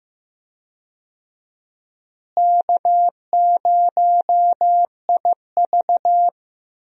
Morse code, heard as a keyed tone, about 15 wpm, 700 Hz. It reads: K0IV
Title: USA Call Signs - 4 Characters - Repeated 2x - 15wpm